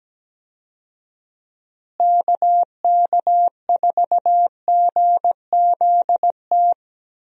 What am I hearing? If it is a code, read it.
Morse code, heard as a keyed tone, about 17 wpm, 700 Hz. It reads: KK4GZT
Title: USA Call Signs - 6 Characters - Repeated 3x - 17wpm